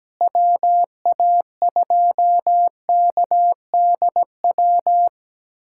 Morse code, heard as a keyed tone, about 17 wpm, 700 Hz. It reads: WA2KDW
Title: USA Callsign DE Callsign - 6 Characters - 17wpm